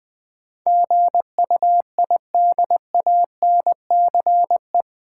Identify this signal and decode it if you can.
Morse code, heard as a keyed tone, about 21 wpm, 700 Hz. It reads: GUIDANCE
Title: Sentences from Top 1600 Words - New Word Drill - 20wpm